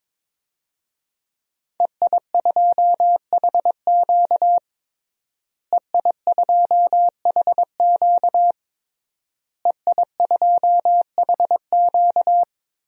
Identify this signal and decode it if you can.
Morse code, heard as a keyed tone, about 22 wpm, 700 Hz. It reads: EI2HQ EI2HQ EI2HQ
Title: DXCC-100 Call Signs - 5 Characters - Repeated 3x - 22wpm